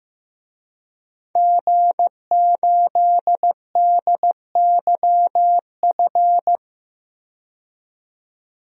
Morse code, heard as a keyed tone, about 15 wpm, 700 Hz. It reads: G8DYF
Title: DXCC-100 Call Signs - 5 Characters - 15wpm